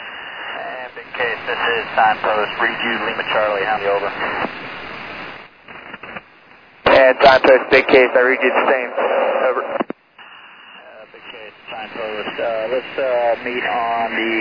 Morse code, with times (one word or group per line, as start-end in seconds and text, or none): none